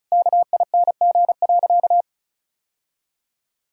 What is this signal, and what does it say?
Morse code, heard as a keyed tone, about 34 wpm, 700 Hz. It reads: KING.
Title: Sentences from Top 1000 Words - Review Part 6 - 35wpm